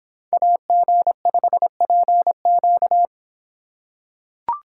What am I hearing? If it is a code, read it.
Morse code, heard as a keyed tone, about 26 wpm, 700 Hz. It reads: AG5PQ